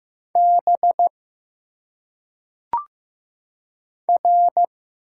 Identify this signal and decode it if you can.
Morse code, heard as a keyed tone, about 15 wpm, 700 Hz. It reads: B R
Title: Single Letters - Phonetic Pronunciation 15wpm